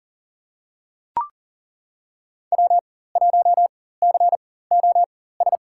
Morse code, heard as a keyed tone, about 37 wpm, spaced as for 18 wpm, 700 Hz. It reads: W1COS
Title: USA Call Signs - 5 Characters - 4x Character Spacing - 40wpm